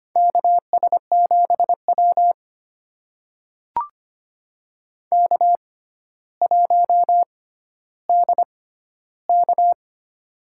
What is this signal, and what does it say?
Morse code, heard as a keyed tone, about 25 wpm, spaced as for 13 wpm, 700 Hz. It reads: KS7W K1DK
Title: USA Call Signs - 4 Characters - 6x Character Spacing - 25wpm